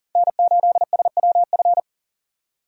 Morse code, heard as a keyed tone, about 38 wpm, 700 Hz. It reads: N8SWF
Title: USA Call Signs - 5 Characters - 40wpm